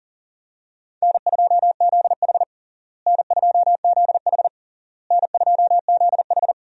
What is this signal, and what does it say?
Morse code, heard as a keyed tone, about 40 wpm, 700 Hz. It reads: N2ZH N2ZH N2ZH
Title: USA Call Signs - 4 Characters - Repeated 3x - 40wpm